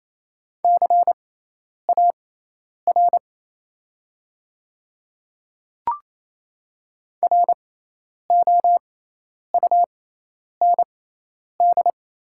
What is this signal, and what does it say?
Morse code, heard as a keyed tone, about 28 wpm, spaced as for 10 wpm, 700 Hz. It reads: CAR ROUND
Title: 6x Character Spacing - Top 300 Words - 28wpm